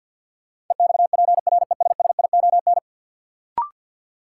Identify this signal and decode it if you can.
Morse code, heard as a keyed tone, about 53 wpm, 700 Hz. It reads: EXPRESSION
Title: Sentences from Top 1200 Words - New Word Drill - 50wpm